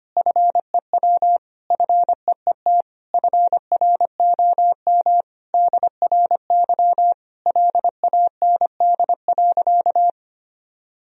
Morse code, heard as a keyed tone, about 25 wpm, 700 Hz. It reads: FEW FEET FROM DRY LAND.
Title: Sentences from Top 1000 Words - Review Part 3 - 25wpm